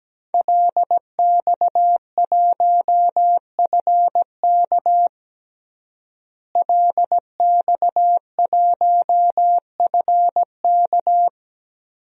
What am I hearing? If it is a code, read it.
Morse code, heard as a keyed tone, about 17 wpm, 700 Hz. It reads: LX1FK LX1FK
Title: DXCC-100 Call Signs - 5 Characters - Repeated 2x - 17wpm